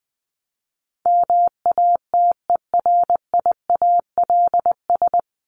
Morse code, heard as a keyed tone, about 20 wpm, 700 Hz. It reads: MATERIALS
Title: ES:S Suffix - Top 100 Words with Base Words 20wpm